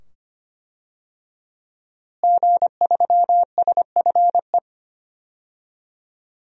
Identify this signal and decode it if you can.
Morse code, heard as a keyed tone, about 25 wpm, 700 Hz. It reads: G3SFE
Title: DXCC-100 Call Signs - 5 Characters - Repeated 2x - 25wpm